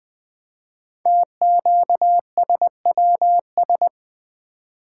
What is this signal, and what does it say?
Morse code, heard as a keyed tone, about 20 wpm, 700 Hz. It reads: TQSWS
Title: Code Groups - 5 Letters - 20wpm